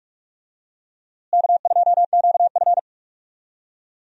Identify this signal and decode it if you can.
Morse code, heard as a keyed tone, about 44 wpm, 700 Hz. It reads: K2QF
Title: USA Call Signs - 4 Characters - Repeated 2x - 45wpm